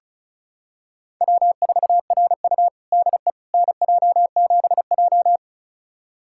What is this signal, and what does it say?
Morse code, heard as a keyed tone, about 36 wpm, 700 Hz. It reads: W4RU DE NJ7J
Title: USA Callsign DE Callsign - 4 Characters - 35wpm